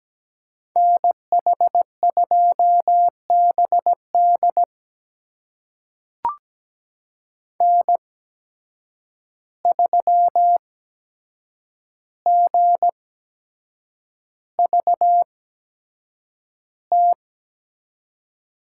Morse code, heard as a keyed tone, about 17 wpm, spaced as for 7 wpm, 700 Hz. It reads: NH2BD N3GVT
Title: USA Call Signs - 5 Characters - 8x Character Spacing - 17wpm